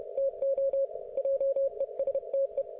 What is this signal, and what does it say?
Morse code, heard as a keyed tone, about 31 wpm, 530 Hz. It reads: TO JESTE